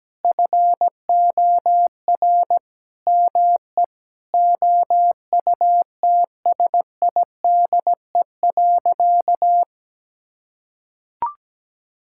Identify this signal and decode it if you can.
Morse code, heard as a keyed tone, about 17 wpm, 700 Hz. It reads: FOR ME OUTSIDE.
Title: Sentences from Top 1000 Words - Review Part 6 - 17wpm